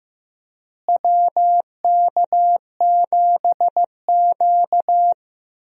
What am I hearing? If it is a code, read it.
Morse code, heard as a keyed tone, about 15 wpm, 700 Hz. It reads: WK7Q
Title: USA Call Signs - 4 Characters - Repeated 3x - 15wpm